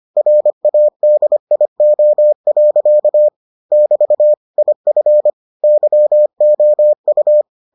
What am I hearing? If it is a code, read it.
Morse code, heard as a keyed tone, about 25 wpm, 600 Hz. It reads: RADIO. <BT> IF YOU